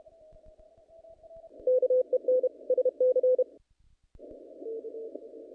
Morse code, heard as a keyed tone, about 32 wpm, 510 Hz. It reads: KEN SC
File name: contests / naqp-cw-k5tr-op-k5pi- -2000